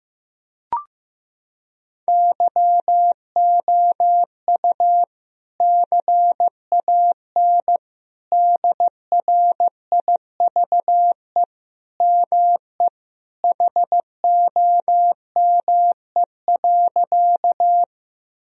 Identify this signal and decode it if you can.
Morse code, heard as a keyed tone, about 15 wpm, 700 Hz. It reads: YOU CAN DRIVE ME HOME.
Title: Sentences from Top 1000 Words - Review Part 2 - 15wpm